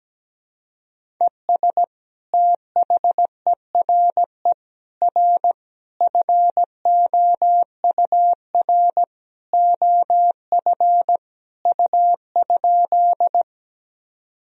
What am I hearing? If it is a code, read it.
Morse code, heard as a keyed tone, about 17 wpm, 700 Hz. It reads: ES THERE R FOUR OF U?
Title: Sentences from Top 300 Words and Abbreviations 17wpm